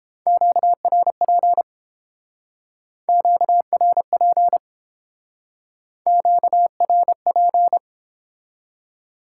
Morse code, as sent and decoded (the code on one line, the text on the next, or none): --.- .-. .--. / --.- .-. .--. / --.- .-. .--.
QRP QRP QRP